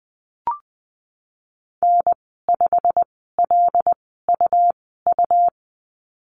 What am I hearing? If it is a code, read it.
Morse code, heard as a keyed tone, about 20 wpm, spaced as for 14 wpm, 700 Hz. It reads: N5LUU
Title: 147 Lesson 24g - Letter y - Call Signs 20wpm